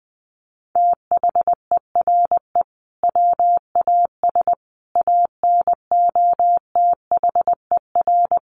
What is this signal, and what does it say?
Morse code, heard as a keyed tone, about 20 wpm, 700 Hz. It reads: THERE WAS ANOTHER
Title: Sentences from Top 1300 Words 20wpm